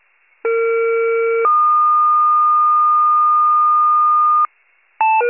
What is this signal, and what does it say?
Two-tone page, heard as a one-second tone, then a three-second tone, about 460 then 1220 Hz. It is heard from 0.4 to 4.5 s.